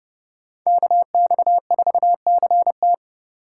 Morse code, heard as a keyed tone, about 30 wpm, 700 Hz. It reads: KX4CT
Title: USA Call Signs - 5 Characters - 3x Character Spacing - 30wpm